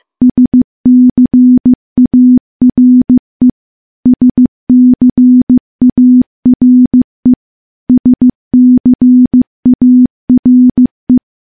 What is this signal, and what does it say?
Morse code, heard as a keyed tone, about 16 wpm, 250 Hz. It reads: SCARE SCARE SCARE
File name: morse-training-15WPM